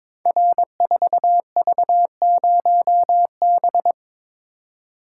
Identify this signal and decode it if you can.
Morse code, heard as a keyed tone, about 22 wpm, 700 Hz. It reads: R4V0B
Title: Code Groups - 5 Letter:Numbers - 22wpm